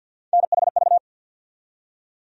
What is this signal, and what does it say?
Morse code, heard as a keyed tone, about 49 wpm, 700 Hz. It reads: NHV
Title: USA Call Sign Postfixes - 3 Letters 50wpm